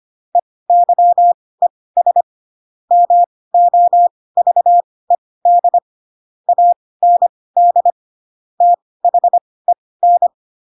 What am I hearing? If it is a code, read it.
Morse code, heard as a keyed tone, about 25 wpm, spaced as for 18 wpm, 700 Hz. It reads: EYES MOVED AND THEN
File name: Pinocchio ch3 12 25 ik1hge